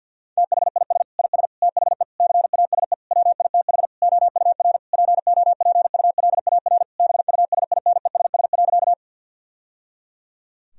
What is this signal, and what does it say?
Morse code, heard as a keyed tone, about 50 wpm, 700 Hz. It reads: THIS IS THE CASE WITH OUR POPULAR BUSINESS.
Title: Sentences from Top 1000 Words - Review Part 3 - 50wpm